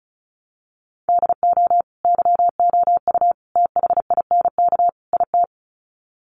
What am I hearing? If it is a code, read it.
Morse code, heard as a keyed tone, about 34 wpm, 700 Hz. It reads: DO YOU THINK IT
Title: Sets of 4 Words 35wpm